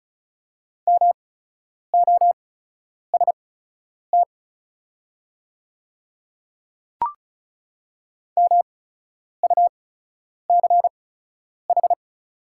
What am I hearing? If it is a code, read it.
Morse code, heard as a keyed tone, about 35 wpm, spaced as for 10 wpm, 700 Hz. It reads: MOST MUCH
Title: 8x Character Spacing - Top 200 Words - 35wpm